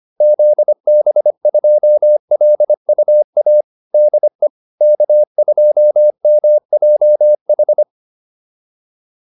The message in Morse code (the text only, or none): ZB2LUA DE K2MJH